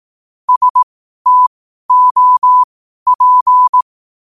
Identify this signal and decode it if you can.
Morse code, heard as a keyed tone, about 18 wpm, spaced as for 12 wpm, 1000 Hz. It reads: STOP
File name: qotdfiles / 12wpm